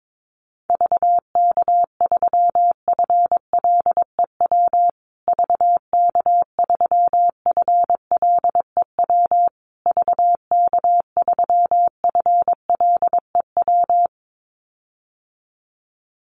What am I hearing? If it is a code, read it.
Morse code, heard as a keyed tone, about 22 wpm, 690 Hz. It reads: VK3FLEW VK3FLEW VK3FLEW
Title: USA Call Signs Repeated 3x 22wpm